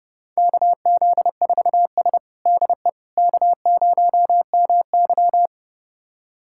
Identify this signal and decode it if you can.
Morse code, heard as a keyed tone, about 30 wpm, 700 Hz. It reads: KZ4S DE K0MY
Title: USA Callsign DE Callsign - 4 Characters - 30wpm